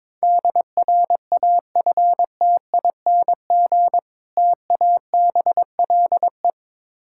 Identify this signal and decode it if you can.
Morse code, heard as a keyed tone, about 22 wpm, 690 Hz. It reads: DRAFTING TABLE
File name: Difficult Words with Hints - 22wpm